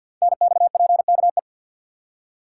Morse code, heard as a keyed tone, about 48 wpm, 700 Hz. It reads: NXPKE
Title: Code Groups - 5 Letters - 50wpm